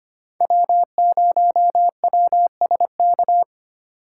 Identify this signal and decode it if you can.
Morse code, heard as a keyed tone, about 25 wpm, 700 Hz. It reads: W0WSK
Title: USA Call Signs - 5 Characters - 2x Character Spacing - 25wpm